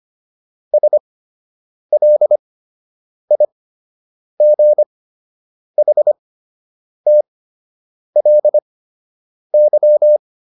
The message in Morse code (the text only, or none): SLIGHTLY